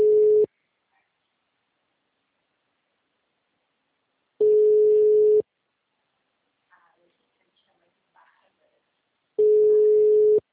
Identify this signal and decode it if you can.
Ringback tone (Europe, 425 Hz), with speech.